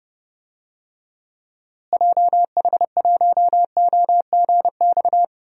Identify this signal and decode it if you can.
Morse code, heard as a keyed tone, about 30 wpm, 700 Hz. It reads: JH1OGX
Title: DXCC-100 Call Signs - 6 Characters - Repeated 3x - 30wpm